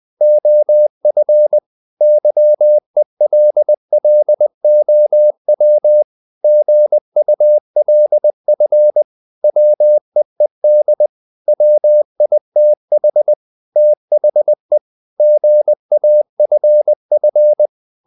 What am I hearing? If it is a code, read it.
Morse code, heard as a keyed tone, about 20 wpm, 600 Hz. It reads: OF YELLOW GULF WEED WITH THE GAFF
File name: the old man and the sea part 2 20wpm